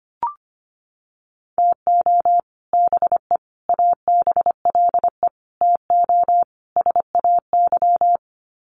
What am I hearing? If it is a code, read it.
Morse code, heard as a keyed tone, about 25 wpm, 700 Hz. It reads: TO BE ABLE TO SAY
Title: Sets of 5 Words Encore2 25wpm